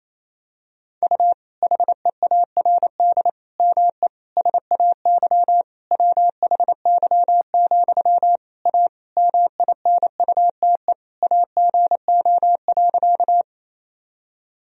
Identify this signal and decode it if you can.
Morse code, heard as a keyed tone, about 28 wpm, 700 Hz. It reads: U HEARD ME SAY WHY, A MINUTE AGO.
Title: Sentences from Top 500 Words and Abbreviations 28wpm